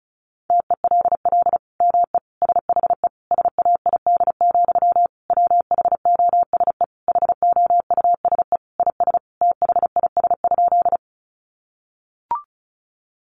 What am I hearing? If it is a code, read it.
Morse code, heard as a keyed tone, about 34 wpm, 700 Hz. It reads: TELL ME SHE SAID, WHOSE HOUSE IS THIS?
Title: Sentences from Top 1200 Words 35wpm